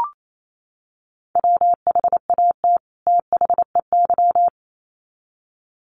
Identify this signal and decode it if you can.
Morse code, heard as a keyed tone, about 28 wpm, 700 Hz. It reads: WHAT THEY
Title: Sets of 2 Words 28wpm